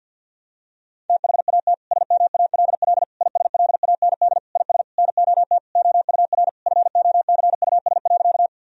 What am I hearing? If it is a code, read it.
Morse code, heard as a keyed tone, about 52 wpm, 700 Hz. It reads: THAT SMALL ISLAND IS NOT OUR FOCUS.
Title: Sentences from Top 1000 Words - Review Part 4 - 50wpm